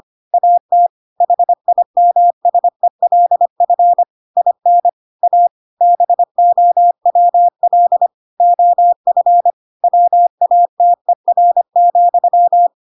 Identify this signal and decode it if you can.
Morse code, heard as a keyed tone, about 25 wpm, 700 Hz. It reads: AT HIMSELF IN A BOWL OF WATER,